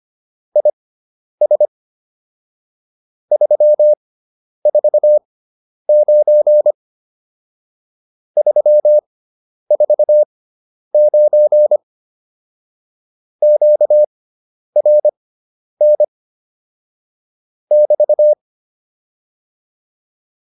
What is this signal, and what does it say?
Morse code, heard as a keyed tone, about 25 wpm, spaced as for 10 wpm, 600 Hz. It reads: IS 349 349 QRN <BT>